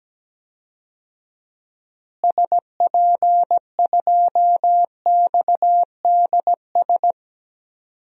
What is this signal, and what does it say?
Morse code, heard as a keyed tone, about 17 wpm, 700 Hz. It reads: SP2XDS
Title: DXCC-100 Call Signs - 6 Characters - Repeated 2x - 17wpm